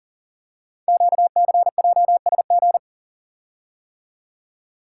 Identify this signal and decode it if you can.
Morse code, heard as a keyed tone, about 38 wpm, 700 Hz. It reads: QCJSG
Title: Code Groups - 5 Letters - 40wpm